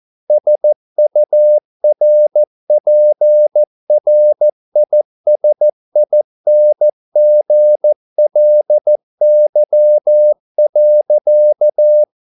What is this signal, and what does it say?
Morse code, heard as a keyed tone, about 14 wpm, 600 Hz. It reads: SURPRISINGLY.